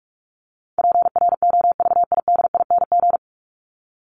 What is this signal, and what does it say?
Morse code, heard as a keyed tone, about 44 wpm, 700 Hz. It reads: PROVIDING
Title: ING Suffix - Top 100 Words with Base Words 45wpm